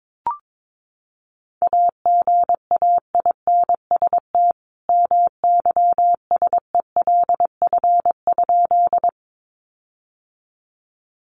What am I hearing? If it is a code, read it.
Morse code, heard as a keyed tone, about 22 wpm, 700 Hz. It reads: AGAINST MYSELF?
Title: Sentences from Top 1300 Words 22wpm